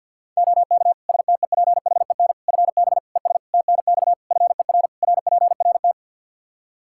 Morse code, heard as a keyed tone, about 51 wpm, 700 Hz. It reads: OK STEPHEN FB ES TNX FER RPRT
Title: Practice QSO - Exchange 2 - Info - 50wpm